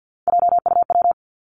Morse code, heard as a keyed tone, about 50 wpm, 700 Hz. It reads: JUW
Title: Three Challenging Letter-Number Combinations 50wpm